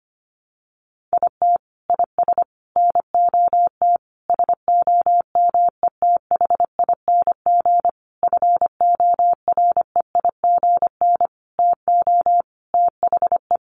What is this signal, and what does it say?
Morse code, heard as a keyed tone, about 25 wpm, 700 Hz. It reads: IT IS NOT SOMETHING FOREIGN TO THE
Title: Sentences from Top 1400 Words 25wpm